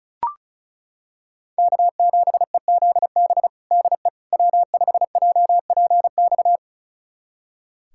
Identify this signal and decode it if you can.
Morse code, heard as a keyed tone, about 34 wpm, 700 Hz. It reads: K7EZB DE W5JPX
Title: USA Callsign DE Callsign - 5 Characters - 35wpm